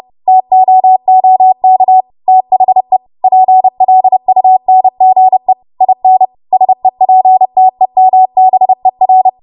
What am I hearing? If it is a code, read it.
Morse code, heard as a keyed tone, about 30 wpm, 750 Hz. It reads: TOOK THE PLUNGE IN SEPTEMBER
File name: Morse / 30WPM